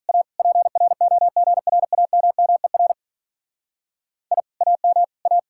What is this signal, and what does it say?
Morse code, heard as a keyed tone, about 45 wpm, 700 Hz. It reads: A PROGRAMMER I AM A